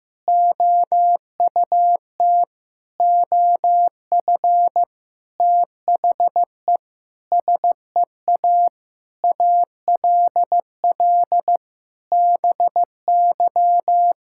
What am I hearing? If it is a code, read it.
Morse code, heard as a keyed tone, about 15 wpm, 700 Hz. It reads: OUT OF THE SEA ALL BY
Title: Sentences from Top 1000 Words - Review Part 5 - 15wpm